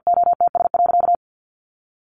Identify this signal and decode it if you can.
Morse code, heard as a keyed tone, about 50 wpm, 700 Hz. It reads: OTS.
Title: A Sound Of Thunder 50wpm